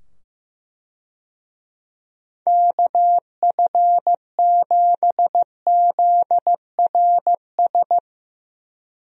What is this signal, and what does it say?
Morse code, heard as a keyed tone, about 15 wpm, 700 Hz. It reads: KF7ZRS
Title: USA Call Signs - 6 Characters - 15wpm